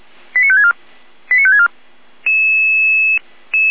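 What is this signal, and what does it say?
Five-tone selective call (ZVEI), heard as quick runs of five tones, tones of about 70 ms. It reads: 89564 89564
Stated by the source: ZVEI five-tone selective call received on an SDR.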